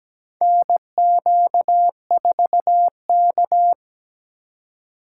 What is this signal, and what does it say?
Morse code, heard as a keyed tone, about 17 wpm, 700 Hz. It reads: NQ4K
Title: USA Call Signs - 4 Characters - 4x Character Spacing - 17wpm